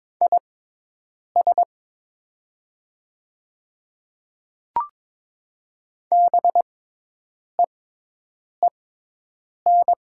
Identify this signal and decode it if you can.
Morse code, heard as a keyed tone, about 22 wpm, spaced as for 8 wpm, 700 Hz. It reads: IS BEEN
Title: 6x Character Spacing - Top 300 Words - 22wpm